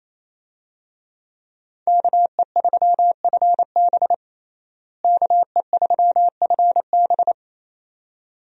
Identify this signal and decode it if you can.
Morse code, heard as a keyed tone, about 28 wpm, 700 Hz. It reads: KE3FB KE3FB
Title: USA Call Signs - 5 Characters - Repeated 2x - 28wpm